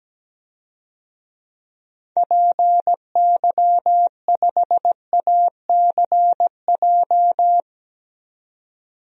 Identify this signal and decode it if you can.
Morse code, heard as a keyed tone, about 17 wpm, 700 Hz. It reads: PY5ACJ
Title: DXCC-100 Call Signs - 6 Characters - 17wpm